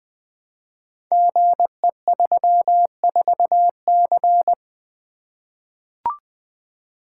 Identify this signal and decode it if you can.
Morse code, heard as a keyed tone, about 20 wpm, 700 Hz. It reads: GE34C
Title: Code Groups - 5 Letter:Numbers - 20wpm